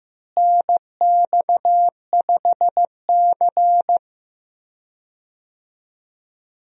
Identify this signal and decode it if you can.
Morse code, heard as a keyed tone, about 15 wpm, 700 Hz. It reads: NX5C
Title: USA Call Signs - 4 Characters - Repeated 2x - 15wpm